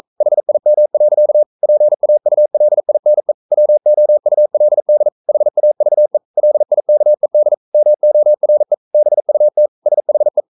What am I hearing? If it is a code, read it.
Morse code, heard as a keyed tone, about 42 wpm, 600 Hz. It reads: HIM. PAULINE WOULD HAVE LIKED MORE BUT SHE
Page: Romanen Never af Ken Follet på engelsk.